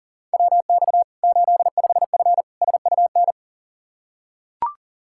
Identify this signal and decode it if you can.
Morse code, heard as a keyed tone, about 40 wpm, 700 Hz. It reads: WX 85F SUN
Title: Practice QSO - Exchange 2 - Info - 40wpm